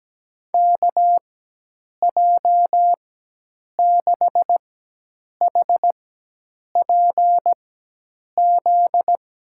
Morse code, heard as a keyed tone, about 17 wpm, spaced as for 8 wpm, 700 Hz. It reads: KJ6HPZ
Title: USA Call Signs - 4x Character Spacing - 17wpm